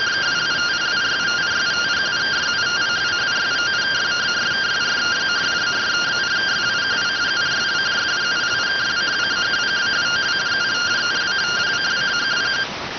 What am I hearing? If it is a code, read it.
Two-tone teleprinter signal, about 170 Hz shift.